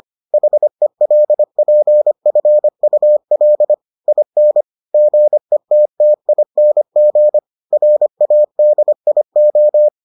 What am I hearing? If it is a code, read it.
Morse code, heard as a keyed tone, about 25 wpm, 600 Hz. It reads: HELPFUL IN GETTING RADIO